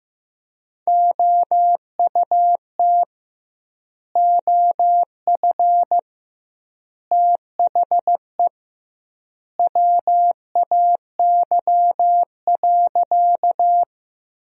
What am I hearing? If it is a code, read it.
Morse code, heard as a keyed tone, about 16 wpm, 700 Hz. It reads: OUT OF THE WAY.
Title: 2x Word-Spacing - Sentences from Top 100 Words - 15wpm